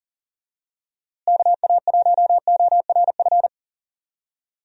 Morse code, heard as a keyed tone, about 40 wpm, 700 Hz. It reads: KA1ORF
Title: USA Call Signs - 6 Characters - Repeated 3x - 40wpm